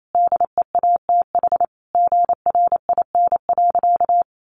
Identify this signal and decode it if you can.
Morse code, heard as a keyed tone, about 28 wpm, 700 Hz. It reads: DEATH GRIN.
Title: A Sound Of Thunder 28wpm